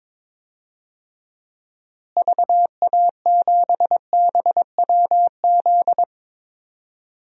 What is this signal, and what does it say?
Morse code, heard as a keyed tone, about 21 wpm, 700 Hz. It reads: VA7BWZ
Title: DXCC-100 Call Signs - 6 Characters - Repeated 2x - 22wpm